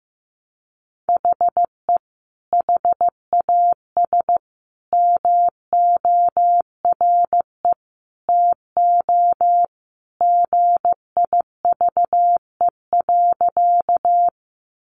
Morse code, heard as a keyed tone, about 15 wpm, 700 Hz. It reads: HE HAS MORE TO GIVE.
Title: Sentences from Top 200 Words 15wpm